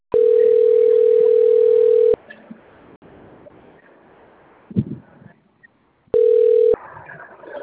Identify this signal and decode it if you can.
Ringback tone (North America), 440 + 480 Hz.